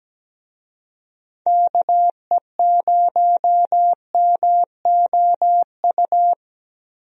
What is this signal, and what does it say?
Morse code, heard as a keyed tone, about 17 wpm, 700 Hz. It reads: KE0MOU
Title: USA Call Signs - 6 Characters - 17wpm